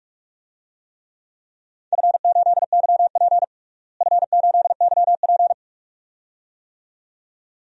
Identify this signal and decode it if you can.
Morse code, heard as a keyed tone, about 42 wpm, 700 Hz. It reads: F8YP F8YP
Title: DXCC-100 Call Signs - 4 Characters - Repeated 2x - 45wpm